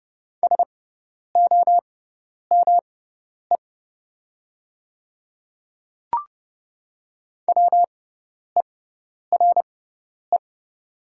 Morse code, heard as a keyed tone, about 29 wpm, spaced as for 10 wpm, 700 Hz. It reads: SOME WERE
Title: 6x Character Spacing - Top 100 Words - 30wpm